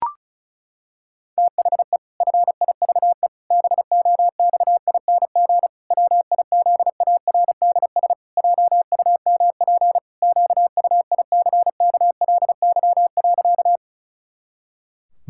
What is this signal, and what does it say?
Morse code, heard as a keyed tone, about 36 wpm, 700 Hz. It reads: THE FIVE BOXING WIZARDS JUMP QUICKLY.
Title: Pangram Challenge v2 - 35wpm